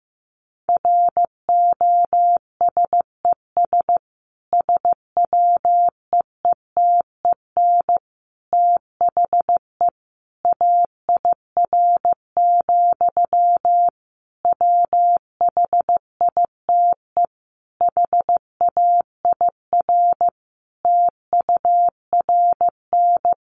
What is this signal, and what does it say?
Morse code, heard as a keyed tone, about 15 wpm, 700 Hz. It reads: ROSES SWEETEN THE AIR, WHITE HAIR TURN
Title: A Sound Of Thunder 15wpm